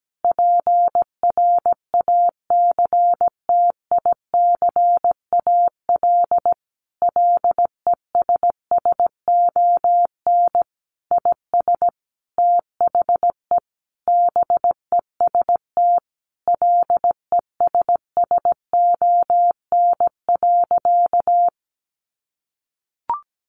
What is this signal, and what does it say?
Morse code, heard as a keyed tone, about 17 wpm, 700 Hz. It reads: PRACTICAL LESSON IS THE BEST LESSON.
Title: Sentences from Top 1400 Words 17wpm